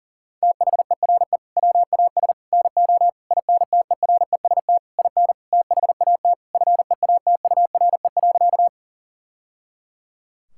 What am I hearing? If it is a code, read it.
Morse code, heard as a keyed tone, about 40 wpm, 700 Hz. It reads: THERE WAS NO INTEREST IN THAT FEATURE.
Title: Sentences from Top 1000 Words - Review Part 5 - 40wpm